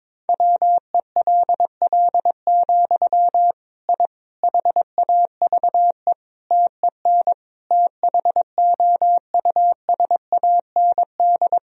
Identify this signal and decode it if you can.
Morse code, heard as a keyed tone, about 22 wpm, 700 Hz. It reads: WELL, I HAVE TEN THOUSAND
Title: Sentences from Top 1000 Words - Review Part 3 - 22wpm